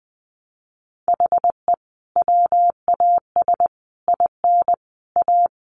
Morse code, heard as a keyed tone, about 20 wpm, 700 Hz. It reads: HE WAS IN A
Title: Sentences from Top 1100 Words 20wpm